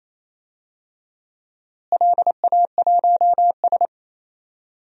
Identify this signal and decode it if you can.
Morse code, heard as a keyed tone, about 28 wpm, 700 Hz. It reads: LA1S
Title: DXCC-100 Call Signs - 4 Characters - Repeated 2x - 28wpm